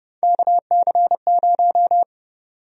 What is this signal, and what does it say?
Morse code, heard as a keyed tone, about 31 wpm, 700 Hz. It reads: KC0
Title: USA Call Sign Prefixes - 30wpm